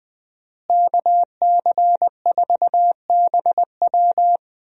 Morse code, heard as a keyed tone, about 20 wpm, 700 Hz. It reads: KC4BW
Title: USA Call Signs - 5 Characters - Repeated 3x - 20wpm